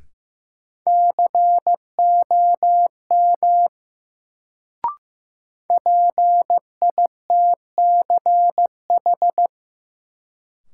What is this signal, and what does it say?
Morse code, heard as a keyed tone, about 15 wpm, 700 Hz. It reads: COM PITCH
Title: Sentences from Top 1300 Words - New Word Drill - 15wpm